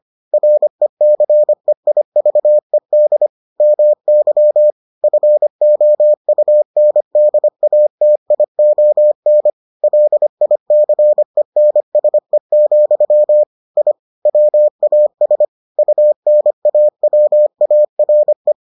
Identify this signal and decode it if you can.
Morse code, heard as a keyed tone, about 25 wpm, 600 Hz. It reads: RECEIVED MY FOUNDATION LICENSE, I WAS UNAWARE